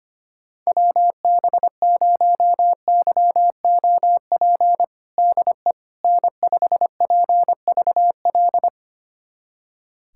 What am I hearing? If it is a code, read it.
Morse code, heard as a keyed tone, about 25 wpm, 700 Hz. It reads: WB0YOP DE N5PVL
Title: USA Callsign DE Callsign 25wpm